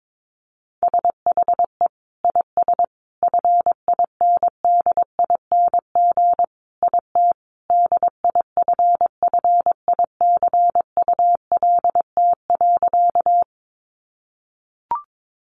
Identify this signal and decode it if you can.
Morse code, heard as a keyed tone, about 22 wpm, 700 Hz. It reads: SHE IS FINDING IT DIFFICULT.
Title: Sentences from Top 1100 Words 22wpm